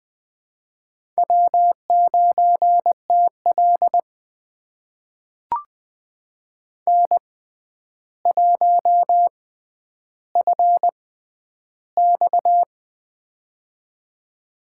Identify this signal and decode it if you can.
Morse code, heard as a keyed tone, about 20 wpm, spaced as for 10 wpm, 700 Hz. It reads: W9TL N1FX
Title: USA Call Signs - 4 Characters - 6x Character Spacing - 20wpm